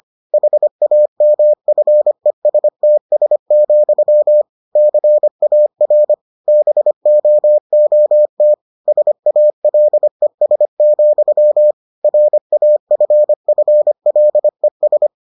Morse code, heard as a keyed tone, about 25 wpm, 600 Hz. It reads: HAMFESTS, CAR BOOT SALES, RAFFLES